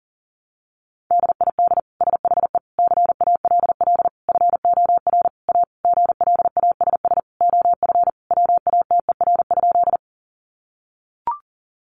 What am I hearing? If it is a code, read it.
Morse code, heard as a keyed tone, about 40 wpm, 700 Hz. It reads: DID SHE CALL FOR A GLASS OF WATER?
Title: Sentences from Top 1300 Words 40wpm